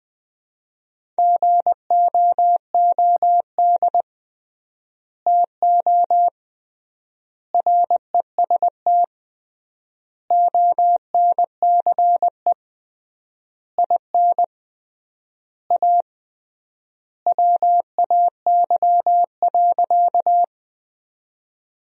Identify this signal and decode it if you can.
Morse code, heard as a keyed tone, about 20 wpm, 700 Hz. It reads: GOOD TO REST ONCE IN A WAY.
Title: 3x Word-Spacing - Sentences from Top 500 Words - 20wpm